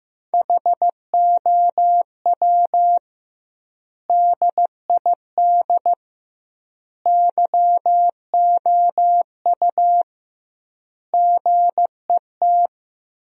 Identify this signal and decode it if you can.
Morse code, heard as a keyed tone, about 15 wpm, 700 Hz. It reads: HOW DID YOU GET
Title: 2x Word-Spacing - Sentences from Top 200 Words - 15wpm